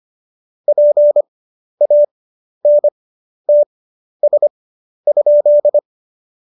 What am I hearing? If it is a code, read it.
Morse code, heard as a keyed tone, about 25 wpm, spaced as for 11 wpm, 590 Hz. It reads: PANTS?
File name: Jokes-25-char-10-eff